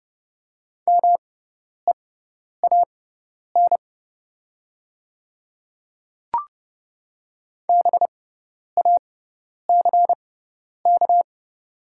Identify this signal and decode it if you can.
Morse code, heard as a keyed tone, about 30 wpm, spaced as for 10 wpm, 690 Hz. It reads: MEAN BACK
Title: 6x Character Spacing - Top 500 Words - 30wpm